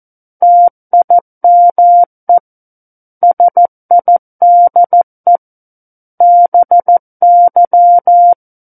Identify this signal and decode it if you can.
Morse code, heard as a keyed tone, about 14 wpm, 700 Hz. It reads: TIME SIDE BY
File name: Book One / 05-Chapter 5 - 14 WPM